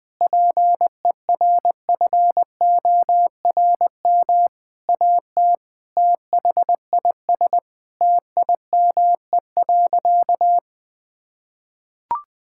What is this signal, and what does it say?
Morse code, heard as a keyed tone, about 21 wpm, 700 Hz. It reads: PERFORM AT THIS TIME.
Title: Sentences from Top 1000 Words - Review Part 5 - 20wpm